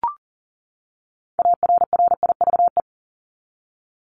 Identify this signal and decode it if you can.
Morse code, heard as a keyed tone, about 40 wpm, 700 Hz. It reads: ARRIVE
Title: ED Suffix - Top 100 Words with Base Words 40wpm